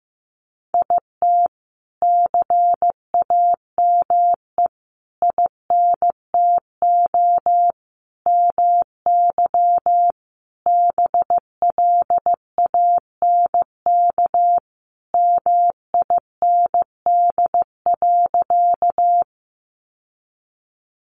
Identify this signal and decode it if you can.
Morse code, heard as a keyed tone, about 14 wpm, 700 Hz. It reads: IT CAME INTO MY BLANK MIND.
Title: Sentences from Top 1500 Words 15wpm